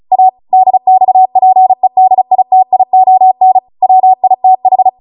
Morse code, heard as a keyed tone, about 36 wpm, 750 Hz. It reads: A DXPEDITION WITH